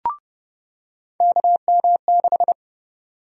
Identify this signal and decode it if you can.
Morse code, heard as a keyed tone, about 30 wpm, 700 Hz. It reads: KM6
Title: USA Call Sign Prefixes - 30wpm